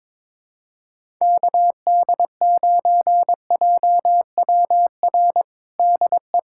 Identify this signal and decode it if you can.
Morse code, heard as a keyed tone, about 22 wpm, 700 Hz. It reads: KD9JWR DE
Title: USA Callsign DE Callsign 22wpm